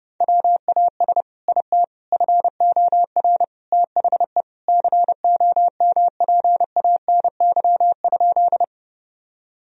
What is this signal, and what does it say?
Morse code, heard as a keyed tone, about 30 wpm, 700 Hz. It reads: WAS IT FOR THE COMPANY?
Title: Sentences from Top 1000 Words - Review Part 3 - 30wpm